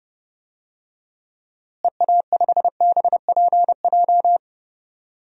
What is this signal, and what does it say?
Morse code, heard as a keyed tone, about 32 wpm, 700 Hz. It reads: EA5BPJ
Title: DXCC-100 Call Signs - 6 Characters - Repeated 3x - 30wpm